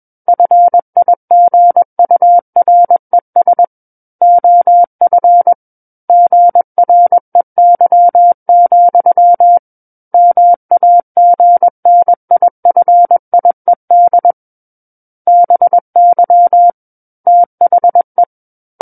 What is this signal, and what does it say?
Morse code, heard as a keyed tone, about 21 wpm, 700 Hz. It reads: FIGURES OF GREY, MAGNIFIED BY THE
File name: Book One / 12-Chapter 12 - 21 WPM